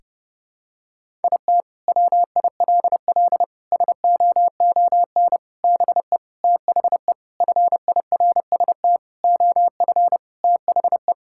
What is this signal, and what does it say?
Morse code, heard as a keyed tone, about 30 wpm, 700 Hz. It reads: IT WILL SOON BE THE FIRST OF THE
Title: Sentences from Top 1000 Words - Review Part 5 - 30wpm